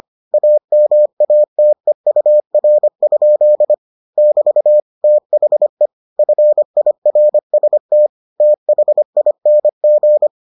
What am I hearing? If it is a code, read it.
Morse code, heard as a keyed tone, about 25 wpm, 600 Hz. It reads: AMATEUR? <BT> THE FIRST THING